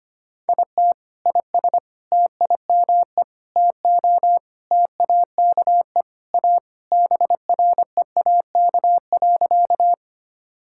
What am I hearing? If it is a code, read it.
Morse code, heard as a keyed tone, about 25 wpm, 700 Hz. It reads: IT IS TIME TO TAKE A BREAK.
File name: practice / 25wpm / Sentences with 4-Word Phrases - 25wpm